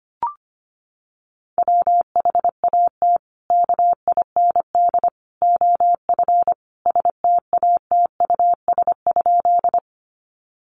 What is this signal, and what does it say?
Morse code, heard as a keyed tone, about 25 wpm, 700 Hz. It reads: WHAT KIND OF STATUS?
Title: Sentences from Top 1400 Words 25wpm